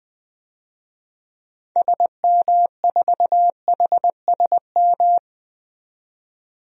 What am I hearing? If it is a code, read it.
Morse code, heard as a keyed tone, about 20 wpm, 700 Hz. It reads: SM4HSM
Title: DXCC-100 Call Signs - 6 Characters - 20wpm